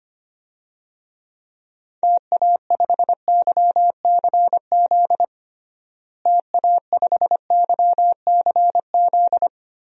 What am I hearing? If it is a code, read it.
Morse code, heard as a keyed tone, about 25 wpm, 700 Hz. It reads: TA5YCZ TA5YCZ
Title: DXCC-100 Call Signs - 6 Characters - Repeated 3x - 25wpm